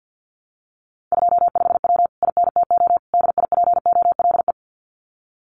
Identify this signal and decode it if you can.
Morse code, heard as a keyed tone, about 50 wpm, 700 Hz. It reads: 25W INTO DIPOLE
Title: Common QSO Phrases Encore 50wpm